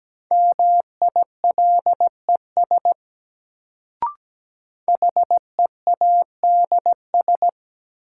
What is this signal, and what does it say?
Morse code, heard as a keyed tone, about 17 wpm, 700 Hz. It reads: MILES HEADS
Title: Sentences from Top 1000 Words - Review Part 3 - New Word Drill - 17wpm